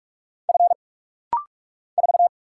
Morse code, heard as a keyed tone, about 44 wpm, 690 Hz. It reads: F 4
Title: Single Letter-Number - Rapid-Fire 45wpm